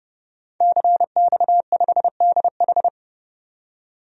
Morse code, heard as a keyed tone, about 32 wpm, 700 Hz. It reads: CX5DH
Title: DXCC-100 Call Signs - 5 Characters - 30wpm